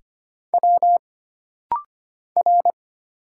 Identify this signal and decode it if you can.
Morse code, heard as a keyed tone, about 25 wpm, 700 Hz. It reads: W R
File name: Single Character (Letter:Number:Punctuation:Prosign) - Rapid-Fire - 25wpm